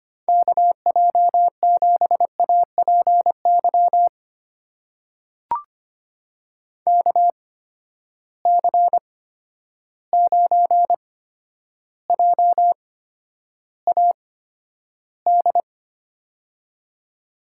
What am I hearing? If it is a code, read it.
Morse code, heard as a keyed tone, about 25 wpm, spaced as for 11 wpm, 700 Hz. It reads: KJ7APY KC9JAD
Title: USA Call Signs - 6 Characters - 8x Character Spacing - 25wpm